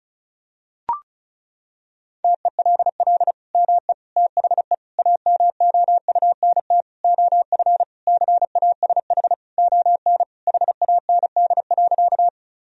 Morse code, heard as a keyed tone, about 36 wpm, 700 Hz. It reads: TELL ME THE AMOUNT OF CASH ON HAND.
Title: Sentences from Top 1000 Words - Review Part 4 - 35wpm